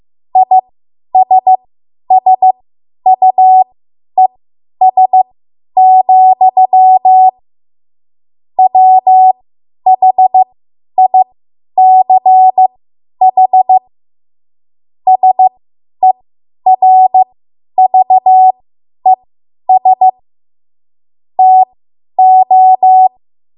Morse code, heard as a keyed tone, about 15 wpm, spaced as for 10 wpm, 750 Hz. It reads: ISSUES, WHICH SERVES TO
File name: Morse / 10WPM